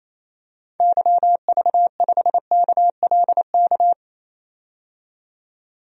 Morse code, heard as a keyed tone, about 27 wpm, 700 Hz. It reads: YV5KLK
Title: DXCC-100 Call Signs - 6 Characters - 28wpm